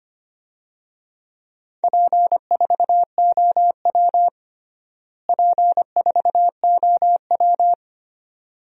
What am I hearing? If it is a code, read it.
Morse code, heard as a keyed tone, about 25 wpm, 700 Hz. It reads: P4OW P4OW
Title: DXCC-100 Call Signs - 4 Characters - Repeated 2x - 25wpm